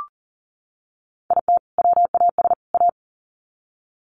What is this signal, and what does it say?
Morse code, heard as a keyed tone, about 40 wpm, 700 Hz. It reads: IT WAS A
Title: Sets of 3 Words 40wpm